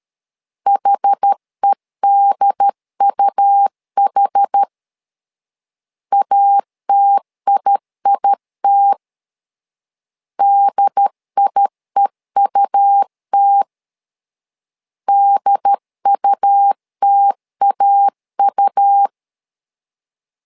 Morse code, heard as a keyed tone, about 13 wpm, 800 Hz. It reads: HEDUH ATIIT DIEUT DUTAU
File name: LEZIONI / 2 UHDN / 2 UHD 12wpm 1540